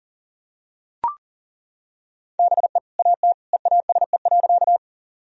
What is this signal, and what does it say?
Morse code, heard as a keyed tone, about 40 wpm, 700 Hz. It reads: BE AT EASE.
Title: Sentences from Top 1000 Words - Review Part 1 - 40wpm